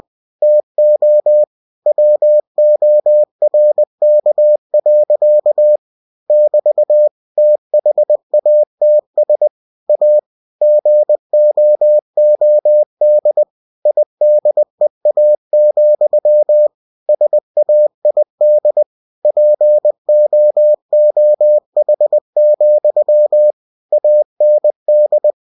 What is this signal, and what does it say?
Morse code, heard as a keyed tone, about 20 wpm, 600 Hz. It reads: TO WORK. <BT> THATS A GOOD IDEA, SAID POOH, AND
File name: cw-books / winnie-the-pooh / winnie-20wpm-page-0083